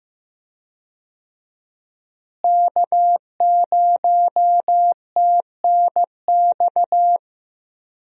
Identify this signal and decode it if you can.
Morse code, heard as a keyed tone, about 15 wpm, 700 Hz. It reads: K0TNX
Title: USA Call Signs - 5 Characters - Repeated 3x - 15wpm